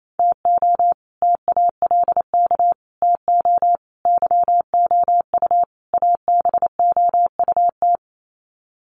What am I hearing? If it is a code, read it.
Morse code, heard as a keyed tone, about 28 wpm, 700 Hz. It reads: TO TALK TO YOU ABOUT
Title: Sets of 5 Words Encore 28wpm